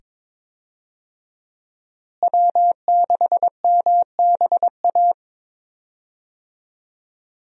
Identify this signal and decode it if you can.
Morse code, heard as a keyed tone, about 22 wpm, 700 Hz. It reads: W6MBA
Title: USA Call Signs - 5 Characters - Repeated 2x - 22wpm